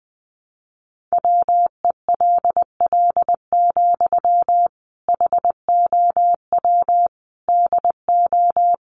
Morse code, heard as a keyed tone, about 20 wpm, 700 Hz. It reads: WELL, HOW DO
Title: Sentences from Top 1100 Words 20wpm